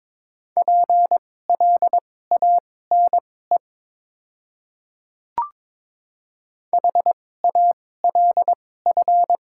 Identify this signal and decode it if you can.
Morse code, heard as a keyed tone, about 22 wpm, spaced as for 16 wpm, 700 Hz. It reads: PLANE HALF
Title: 2x Character Spacing - Top 500 Words - 22wpm